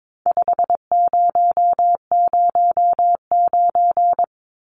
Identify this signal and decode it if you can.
Morse code, heard as a keyed tone, about 22 wpm, 700 Hz. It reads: 5009
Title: Numbers - 4-Digit 22wpm